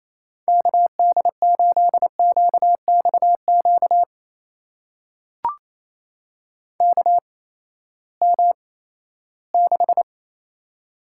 Morse code, heard as a keyed tone, about 27 wpm, spaced as for 16 wpm, 700 Hz. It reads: KD8QXQ KM6
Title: USA Call Signs - 6 Characters - 8x Character Spacing - 28wpm